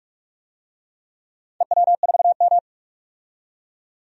Morse code, heard as a keyed tone, about 46 wpm, 700 Hz. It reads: EW4M
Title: DXCC-100 Call Signs - 4 Characters - 45wpm